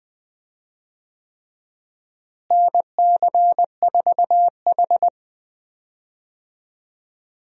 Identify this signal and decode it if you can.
Morse code, heard as a keyed tone, about 20 wpm, 700 Hz. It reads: NC4H
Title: USA Call Signs - 4 Characters - Repeated 3x - 20wpm